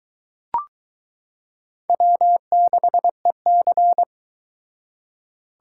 Morse code, heard as a keyed tone, about 23 wpm, 700 Hz. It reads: W6EC